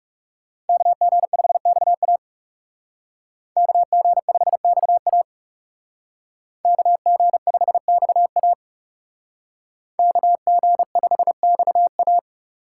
Morse code, heard as a keyed tone, about 38 wpm, 700 Hz. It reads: KG5XA KG5XA KG5XA KG5XA